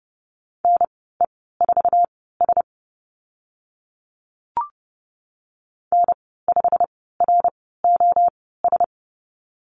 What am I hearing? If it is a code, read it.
Morse code, heard as a keyed tone, about 30 wpm, spaced as for 17 wpm, 690 Hz. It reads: NE4S N5ROS
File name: Lesson 15g - Number 5 - Call Signs 30wpm